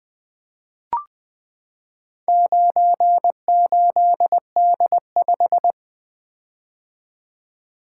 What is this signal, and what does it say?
Morse code, heard as a keyed tone, about 20 wpm, 700 Hz. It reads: 98D5
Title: Four Letter-Number Combinations - 20wpm